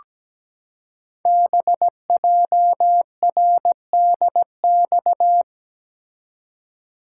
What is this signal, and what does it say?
Morse code, heard as a keyed tone, about 17 wpm, 700 Hz. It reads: BJRDX
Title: Code Groups - 5 Letters - 17wpm